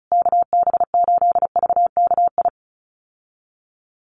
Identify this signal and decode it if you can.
Morse code, heard as a keyed tone, about 34 wpm, 700 Hz. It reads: KB8VKI
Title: USA Call Signs Repeated 3x 35wpm